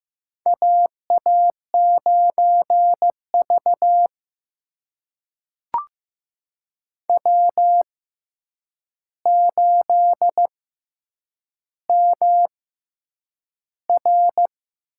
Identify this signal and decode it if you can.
Morse code, heard as a keyed tone, about 15 wpm, spaced as for 8 wpm, 700 Hz. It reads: AA9V W8MR
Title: USA Call Signs - 4 Characters - 6x Character Spacing - 15wpm